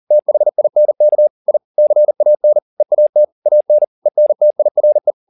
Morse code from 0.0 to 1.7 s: THINK I

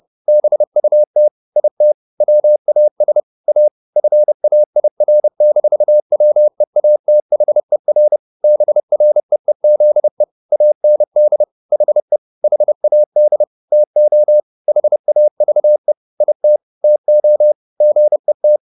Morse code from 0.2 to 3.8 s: BUT IT WAS A

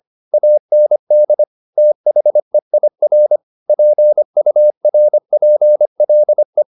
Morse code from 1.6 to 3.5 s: THEIR

